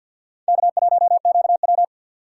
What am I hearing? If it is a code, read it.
Morse code, heard as a keyed tone, about 64 wpm, 700 Hz. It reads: K1QW